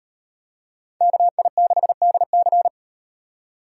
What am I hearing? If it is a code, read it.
Morse code, heard as a keyed tone, about 35 wpm, 700 Hz. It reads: KI6DC